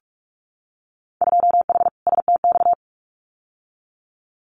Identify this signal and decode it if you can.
Morse code, heard as a keyed tone, about 46 wpm, 700 Hz. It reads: 2H STX
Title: Winter Field Day Exchanges Encore 45wpm